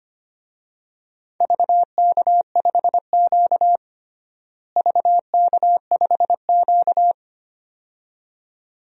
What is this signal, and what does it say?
Morse code, heard as a keyed tone, about 25 wpm, 700 Hz. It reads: VK5Q VK5Q
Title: DXCC-100 Call Signs - 4 Characters - Repeated 2x - 25wpm